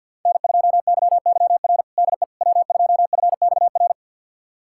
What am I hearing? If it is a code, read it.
Morse code, heard as a keyed tone, about 51 wpm, 700 Hz. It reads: N2YYR DE W2FXR
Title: USA Callsign DE Callsign - 5 Characters - 50wpm